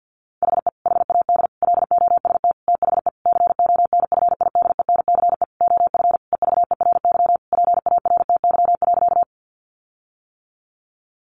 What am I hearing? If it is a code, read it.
Morse code, heard as a keyed tone, about 50 wpm, 700 Hz. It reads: HE HAD LOST THE CONFIDENCE OF EVERY PARTY.
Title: Sentences from Top 1500 Words 50wpm